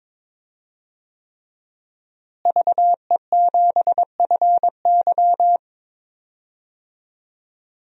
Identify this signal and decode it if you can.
Morse code, heard as a keyed tone, about 22 wpm, 700 Hz. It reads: VE7FY
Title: DXCC-100 Call Signs - 5 Characters - Repeated 3x - 22wpm